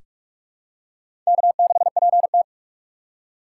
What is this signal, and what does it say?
Morse code, heard as a keyed tone, about 43 wpm, 700 Hz. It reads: K6PT
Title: USA Call Signs - 4 Characters - 8x Character Spacing - 45wpm